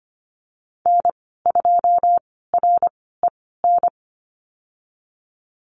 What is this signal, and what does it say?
Morse code, heard as a keyed tone, about 25 wpm, spaced as for 16 wpm, 700 Hz. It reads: N2REN
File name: Lesson 15g - Number 5 - Call Signs 25wpm